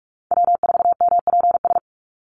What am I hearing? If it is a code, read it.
Morse code, heard as a keyed tone, about 44 wpm, 700 Hz. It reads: W4MPS
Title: NA SOTA Chaser Callsigns 45wpm